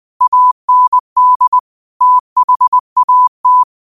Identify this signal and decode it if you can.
Morse code, heard as a keyed tone, about 20 wpm, 1000 Hz. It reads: AND THAT
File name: qotdfiles / 20wpm